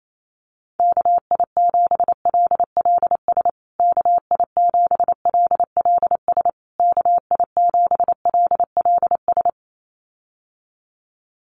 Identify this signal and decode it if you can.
Morse code, heard as a keyed tone, about 28 wpm, 700 Hz. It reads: KI7LLS KI7LLS KI7LLS
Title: USA Call Signs Repeated 3x 28wpm